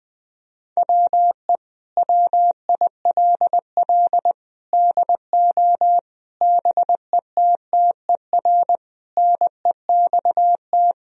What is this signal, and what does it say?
Morse code, heard as a keyed tone, about 20 wpm, 700 Hz. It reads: WE WILL DO BETTER NEXT
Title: Sentences from Top 1000 Words - Review Part 1 - 20wpm